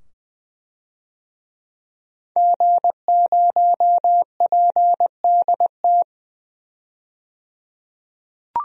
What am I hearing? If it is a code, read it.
Morse code, heard as a keyed tone, about 21 wpm, 700 Hz. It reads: G0PDT
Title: DXCC-100 Call Signs - 5 Characters - 20wpm